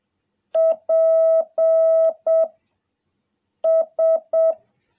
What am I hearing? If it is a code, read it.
Morse code, heard as a keyed tone, about 7 wpm, 650 Hz. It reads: P S
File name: jh8cbh / 2021cw